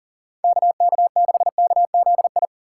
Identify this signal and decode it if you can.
Morse code, heard as a keyed tone, about 37 wpm, 700 Hz. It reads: KK6KZI
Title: USA Call Signs - 6 Characters - 3x Character Spacing - 40wpm